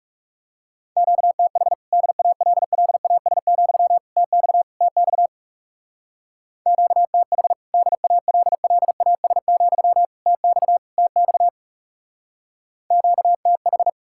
Morse code, heard as a keyed tone, about 36 wpm, 700 Hz. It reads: QTH DALLAS, TX TX QTH DALLAS, TX TX QTH
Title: Speed-Racing Common QSO Phrases - 30wpm